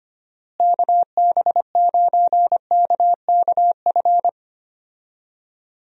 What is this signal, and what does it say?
Morse code, heard as a keyed tone, about 25 wpm, 700 Hz. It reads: KB9KKF
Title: USA Call Signs - 6 Characters - Repeated 2x - 25wpm